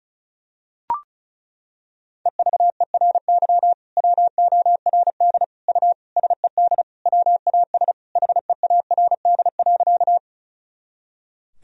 Morse code, heard as a keyed tone, about 34 wpm, 700 Hz. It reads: EVERY WORD U SED WAS HEARD.
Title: Sentences from Top 400 Words and Abbreviations 35wpm